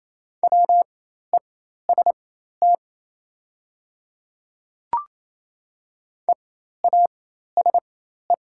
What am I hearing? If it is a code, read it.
Morse code, heard as a keyed tone, about 28 wpm, spaced as for 13 wpm, 700 Hz. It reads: WEST EASE
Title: 4x Character Spacing - Top 500 Words - 28wpm